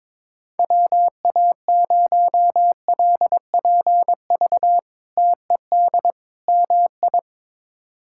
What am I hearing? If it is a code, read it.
Morse code, heard as a keyed tone, about 22 wpm, 700 Hz. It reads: WA0LPV TED MI
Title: K1USN SST QSO Practice - 22wpm